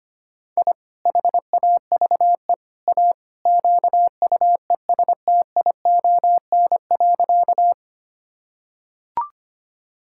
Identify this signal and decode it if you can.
Morse code, heard as a keyed tone, about 25 wpm, 700 Hz. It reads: I HAVE A QUESTION.
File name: practice / 25wpm / Sentences with 3-Word Phrases - 25wpm